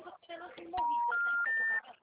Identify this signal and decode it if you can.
Special information tone, with speech.